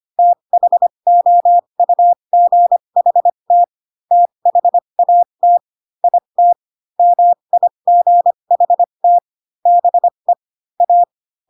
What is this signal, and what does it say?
Morse code, heard as a keyed tone, about 25 wpm, spaced as for 22 wpm, 700 Hz. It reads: THOUGHT THAT IT MIGHT BE A
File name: Pinocchio / Pinocchio ch3 18 25 ik1hge